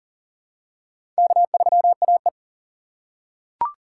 Morse code, heard as a keyed tone, about 40 wpm, 700 Hz. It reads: K3AE
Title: USA Call Signs - 4 Characters - 2x Character Spacing - 40wpm